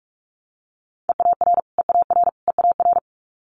Morse code, heard as a keyed tone, about 46 wpm, 700 Hz. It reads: EAR EAR EAR
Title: Multi-letter Phonemes Repeated 3x 45wpm